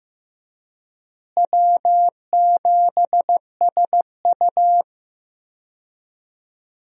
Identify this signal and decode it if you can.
Morse code, heard as a keyed tone, about 15 wpm, 700 Hz. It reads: W7SU
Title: USA Call Signs - 4 Characters - Repeated 3x - 15wpm